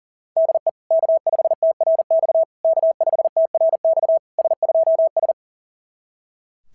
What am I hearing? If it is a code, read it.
Morse code, heard as a keyed tone, about 40 wpm, 690 Hz. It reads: DE K5TRX K5TRX S2S
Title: SOTA QSO Practice 40wpm